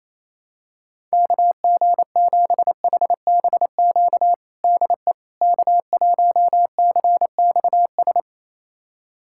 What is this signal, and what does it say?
Morse code, heard as a keyed tone, about 28 wpm, 700 Hz. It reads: KG7HBQ DE K1CXS
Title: USA Callsign DE Callsign 28wpm